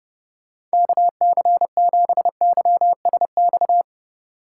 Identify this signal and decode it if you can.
Morse code, heard as a keyed tone, about 30 wpm, 700 Hz. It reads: KC7YSX
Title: USA Call Signs - 6 Characters - 3x Character Spacing - 30wpm